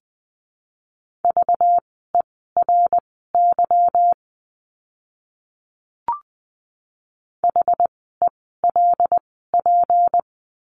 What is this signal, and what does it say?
Morse code, heard as a keyed tone, about 20 wpm, spaced as for 14 wpm, 700 Hz. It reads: VERY HELP
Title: 237 Lesson 38f - Letter q - Words 20wpm